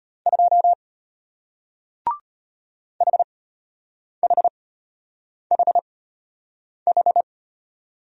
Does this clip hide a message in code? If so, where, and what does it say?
Yes: Morse code from 0.3 to 7.2 s, heard as a keyed tone, about 33 wpm, spaced as for 8 wpm, 700 Hz. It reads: 2 HHHH